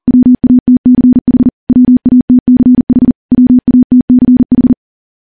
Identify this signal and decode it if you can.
Morse code, heard as a keyed tone, about 42 wpm, 250 Hz. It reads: WATCH WATCH WATCH
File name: morse / morse-training-40WPM